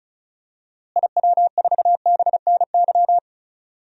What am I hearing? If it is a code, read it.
Morse code, heard as a keyed tone, about 36 wpm, 700 Hz. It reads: IW4BNY
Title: DXCC-100 Call Signs - 6 Characters - Repeated 3x - 35wpm